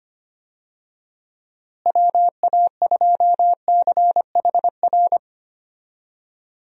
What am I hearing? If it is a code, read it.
Morse code, heard as a keyed tone, about 25 wpm, 700 Hz. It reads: WA2CHR
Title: USA Call Signs - 6 Characters - Repeated 2x - 25wpm